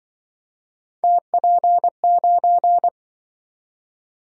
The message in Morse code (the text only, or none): TP9